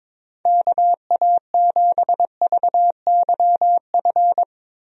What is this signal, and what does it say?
Morse code, heard as a keyed tone, about 22 wpm, 700 Hz. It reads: KA7VYF
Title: USA Call Signs - 6 Characters - Repeated 3x - 22wpm